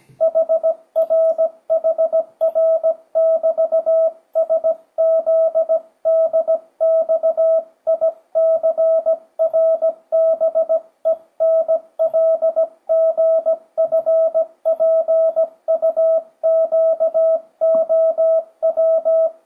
Morse code, heard as a keyed tone, about 17 wpm, 650 Hz. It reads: HRHR<BT>SZDXICRBENLGFPUQOW
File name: 2025cw / speedup / 17wpm